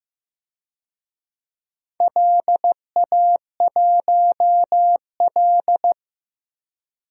Morse code, heard as a keyed tone, about 15 wpm, 700 Hz. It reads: LA1L
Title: DXCC-100 Call Signs - 4 Characters - 15wpm